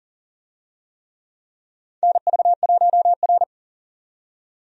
Morse code, heard as a keyed tone, about 40 wpm, 700 Hz. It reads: NV1R
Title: USA Call Signs - 4 Characters - 40wpm